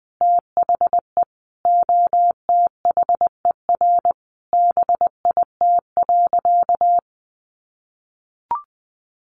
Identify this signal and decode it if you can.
Morse code, heard as a keyed tone, about 20 wpm, 700 Hz. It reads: THE OTHER BIT.
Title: Sentences from Top 900 Words 20wpm